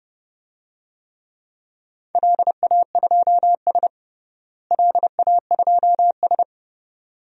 Morse code, heard as a keyed tone, about 30 wpm, 700 Hz. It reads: LA2S LA2S
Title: DXCC-100 Call Signs - 4 Characters - Repeated 2x - 30wpm